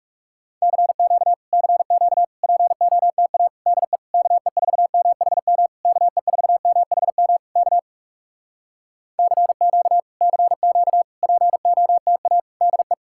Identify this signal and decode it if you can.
Morse code, heard as a keyed tone, about 37 wpm, 700 Hz. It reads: CQ CQ POTA DE KE4MHM KE4MHM K CQ CQ POTA DE
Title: Speed-Racing Common QSO Phrases Encore - 30wpm